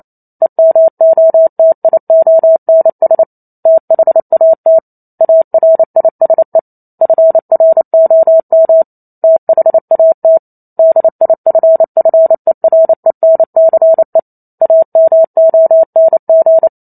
Morse code, heard as a keyed tone, about 28 wpm, 650 Hz. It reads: EMOTIONS THAT ARISE FROM THAT DIFFERENCE AMONG